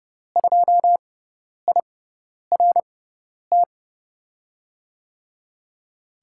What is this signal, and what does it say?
Morse code, heard as a keyed tone, about 30 wpm, spaced as for 10 wpm, 700 Hz. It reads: 2IRT
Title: USA Call Signs - 6 Characters - 6x Character Spacing - 30wpm